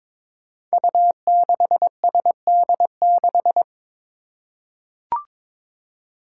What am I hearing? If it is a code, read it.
Morse code, heard as a keyed tone, about 22 wpm, 700 Hz. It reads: U6SD6
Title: Five Challenging Letter-Number Combinations 22wpm